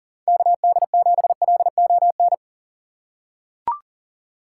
Morse code, heard as a keyed tone, about 40 wpm, 700 Hz. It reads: KD7LON
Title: USA Call Signs - 6 Characters - 4x Character Spacing - 40wpm